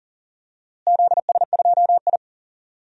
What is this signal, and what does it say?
Morse code, heard as a keyed tone, about 40 wpm, 700 Hz. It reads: ZS2I
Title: DXCC-100 Call Signs - 4 Characters - Repeated 2x - 40wpm